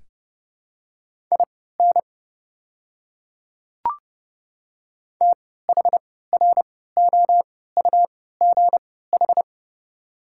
Morse code, heard as a keyed tone, about 30 wpm, spaced as for 17 wpm, 700 Hz. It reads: IN THROUGH
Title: 3x Character Spacing - Top 500 Words - 30wpm